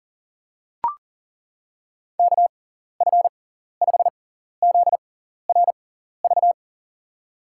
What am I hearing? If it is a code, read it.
Morse code, heard as a keyed tone, about 40 wpm, spaced as for 14 wpm, 700 Hz. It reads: KF5ZRV
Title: USA Call Signs - 6 Characters - 6x Character Spacing - 40wpm